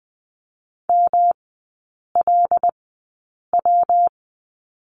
Morse code, heard as a keyed tone, about 20 wpm, spaced as for 8 wpm, 700 Hz. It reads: MLW
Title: 138 Lesson 23e - Letter f - Three Characters 20wpm